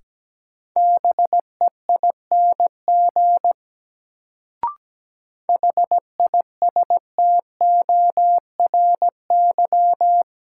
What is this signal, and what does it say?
Morse code, heard as a keyed tone, about 17 wpm, 700 Hz. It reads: BEING HISTORY
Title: Sentences from Top 600 Words - New Word Drill - 17wpm